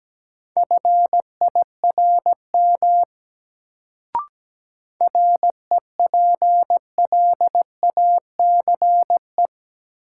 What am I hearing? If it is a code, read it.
Morse code, heard as a keyed tone, about 17 wpm, 700 Hz. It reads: FIRM REPLACE
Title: Sentences from Top 1400 Words - New Word Drill Alt - 17wpm